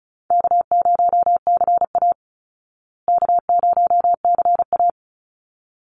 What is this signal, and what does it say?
Morse code, heard as a keyed tone, about 34 wpm, 700 Hz. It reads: K0CA K0CA
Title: Easy Call Signs Repeated 2x - 35wpm